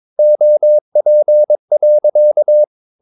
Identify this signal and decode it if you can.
Morse code, heard as a keyed tone, about 22 wpm, 600 Hz. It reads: OP.
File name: Ferie QRS 22wpm